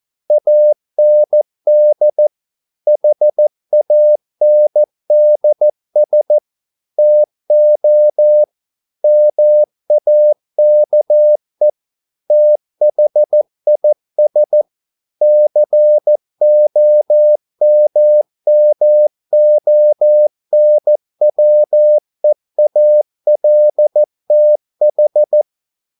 Morse code, heard as a keyed tone, about 14 wpm, 600 Hz. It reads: AND HANDS TO MAKE THIS COMMONWEALTH